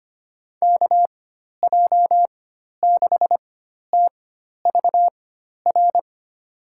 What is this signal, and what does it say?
Morse code, heard as a keyed tone, about 25 wpm, spaced as for 12 wpm, 700 Hz. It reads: KJ6TVR
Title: USA Call Signs - 6 Characters - 4x Character Spacing - 25wpm